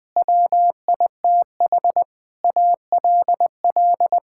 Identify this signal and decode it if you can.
Morse code, heard as a keyed tone, about 20 wpm, 700 Hz. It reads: WITH ALL
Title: Sets of 9 Words - 20wpm